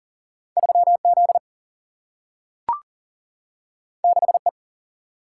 Morse code, heard as a keyed tone, about 38 wpm, 700 Hz. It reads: 3Z 6E
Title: Single-Letter Single-Number 40wpm